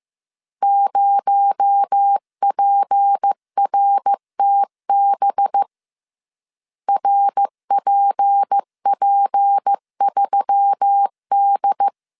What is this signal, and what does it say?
Morse code, heard as a keyed tone, about 15 wpm, 800 Hz. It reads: 0PRTB RPP3D
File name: CW / DET / 1 6 1500C S C 14wpm 2951